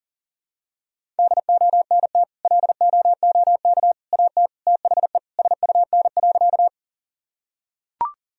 Morse code, heard as a keyed tone, about 40 wpm, 700 Hz. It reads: DONT LOOK AT THE SUN.
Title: Sentences from Top 1000 Words - Review Part 1 - 40wpm